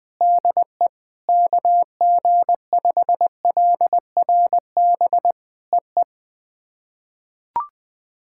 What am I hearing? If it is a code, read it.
Morse code, heard as a keyed tone, about 20 wpm, 700 Hz. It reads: DE KG5LRB EE
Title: POTA QSO Practice - 20wpm